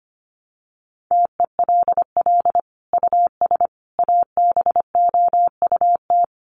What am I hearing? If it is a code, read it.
Morse code, heard as a keyed tone, about 25 wpm, 700 Hz. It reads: TELL US ABOUT
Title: Sentences from Top 1500 Words 25wpm